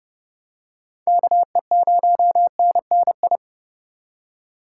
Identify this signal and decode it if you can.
Morse code, heard as a keyed tone, about 30 wpm, 700 Hz. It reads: KE0NNI
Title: USA Call Signs - 6 Characters - 2x Character Spacing - 30wpm